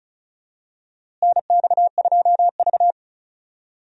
Morse code, heard as a keyed tone, about 36 wpm, 700 Hz. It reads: NX2V
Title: USA Call Signs - 4 Characters - Repeated 3x - 35wpm